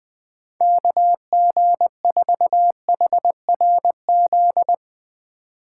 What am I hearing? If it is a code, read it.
Morse code, heard as a keyed tone, about 20 wpm, 700 Hz. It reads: KG4HRZ
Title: Speed Racing - USA Call Signs 17wpm